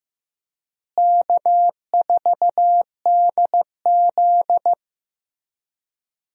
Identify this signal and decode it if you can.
Morse code, heard as a keyed tone, about 15 wpm, 700 Hz. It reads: K4DZ
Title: USA Call Signs - 4 Characters - Repeated 2x - 15wpm